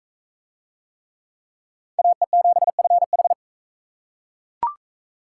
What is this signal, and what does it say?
Morse code, heard as a keyed tone, about 41 wpm, 700 Hz. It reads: AE7FH